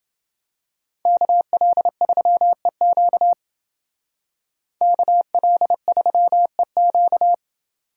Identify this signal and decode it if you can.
Morse code, heard as a keyed tone, about 28 wpm, 700 Hz. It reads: KL3EQ KL3EQ